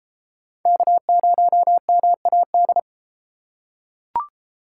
Morse code, heard as a keyed tone, about 32 wpm, 700 Hz. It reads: K0MAD